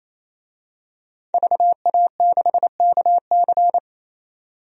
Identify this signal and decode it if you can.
Morse code, heard as a keyed tone, about 27 wpm, 690 Hz. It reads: VA6KC
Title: DXCC-100 Call Signs - 5 Characters - 28wpm